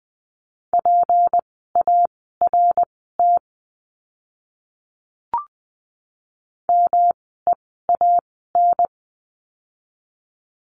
Morse code, heard as a keyed tone, about 20 wpm, spaced as for 14 wpm, 700 Hz. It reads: PART MEAN
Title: 237 Lesson 38f - Letter q - Words 20wpm